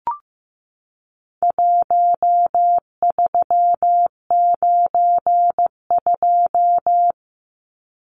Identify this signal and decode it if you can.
Morse code, heard as a keyed tone, about 15 wpm, 700 Hz. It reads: 1392
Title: Numbers - Contest Style - 4-Digit 15wpm